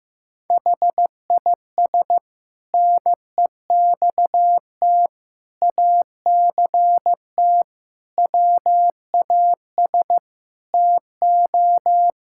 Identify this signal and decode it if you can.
Morse code, heard as a keyed tone, about 15 wpm, 700 Hz. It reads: HIS NEXT ACT WAS TO
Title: Sentences from Top 1000 Words - Review Part 5 - 15wpm